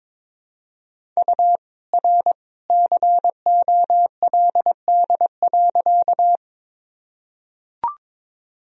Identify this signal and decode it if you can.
Morse code, heard as a keyed tone, about 22 wpm, 700 Hz. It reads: U R COLD.
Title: Sentences from Top 400 Words and Abbreviations 22wpm